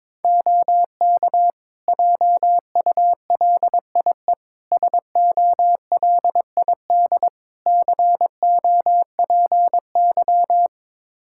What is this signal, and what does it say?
Morse code, heard as a keyed tone, about 21 wpm, 700 Hz. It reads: OK JULIE SOLID COPY
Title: Practice QSO - Exchange 3 - More Info - 22wpm